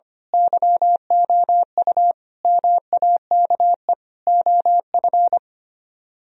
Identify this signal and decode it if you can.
Morse code, heard as a keyed tone, about 25 wpm, 700 Hz. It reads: YOU MAKE OF
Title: Speed-Racing - Sets of 5 Words 25wpm